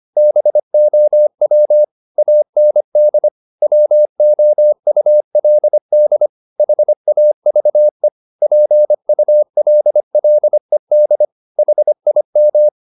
Morse code, heard as a keyed tone, about 25 wpm, 600 Hz. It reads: BOW AND WOULD HAVE PULLED HIM
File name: the old man and the sea part 1 25wpm